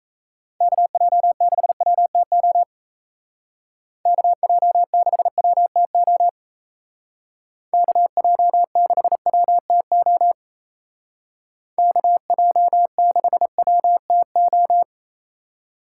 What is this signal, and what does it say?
Morse code, heard as a keyed tone, about 34 wpm, 700 Hz. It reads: KJ6WTO KJ6WTO KJ6WTO KJ6WTO
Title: Speed Racing - USA Call Signs 28wpm